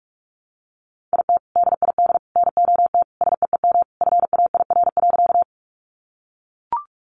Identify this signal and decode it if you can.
Morse code, heard as a keyed tone, about 44 wpm, 700 Hz. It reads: IT DID NOT SEEM FAIR.
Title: Sentences from Top 1300 Words 45wpm